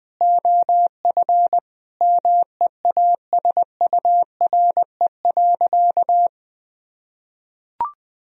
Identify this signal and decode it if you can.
Morse code, heard as a keyed tone, about 20 wpm, 700 Hz. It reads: OF MEASURE.
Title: Sentences from Top 1000 Words - Review Part 3 - 20wpm